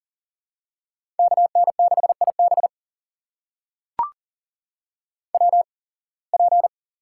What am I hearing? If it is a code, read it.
Morse code, heard as a keyed tone, about 40 wpm, spaced as for 26 wpm, 700 Hz. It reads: KN6IB WP
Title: USA Call Signs - 5 Characters - 8x Character Spacing - 40wpm